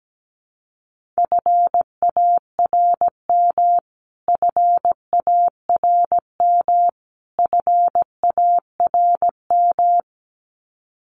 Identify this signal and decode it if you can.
Morse code, heard as a keyed tone, about 17 wpm, 700 Hz. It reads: FARM FARM FARM
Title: Top 500 Words Repeated 17wpm